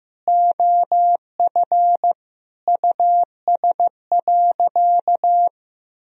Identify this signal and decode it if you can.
Morse code, heard as a keyed tone, about 15 wpm, 700 Hz. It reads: OF US.
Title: Sentences from Top 300 Words and Abbreviations 15wpm